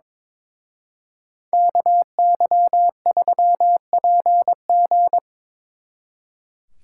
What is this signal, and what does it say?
Morse code, heard as a keyed tone, about 24 wpm, 700 Hz. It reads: KY3PG
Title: Speed Racing - USA 2x2 Call Signs 22wpm